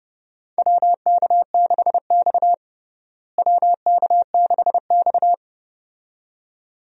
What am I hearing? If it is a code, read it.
Morse code, heard as a keyed tone, about 32 wpm, 700 Hz. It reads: WK6X WK6X
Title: USA Call Signs - 4 Characters - Repeated 2x - 30wpm